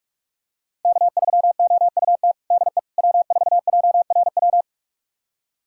Morse code, heard as a keyed tone, about 47 wpm, 700 Hz. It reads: K3OUT DE W4JRW
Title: USA Callsign DE Callsign - 5 Characters - 45wpm